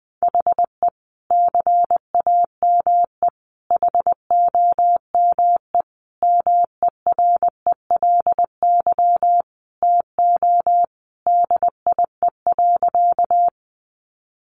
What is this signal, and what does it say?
Morse code, heard as a keyed tone, about 20 wpm, 700 Hz. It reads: HE CAME HOME MERELY TO DIE.
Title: Sentences from Top 1400 Words 20wpm